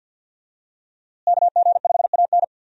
Morse code, heard as a keyed tone, about 51 wpm, 700 Hz. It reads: KG5AN
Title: USA Call Signs - 5 Characters - Repeated 2x - 50wpm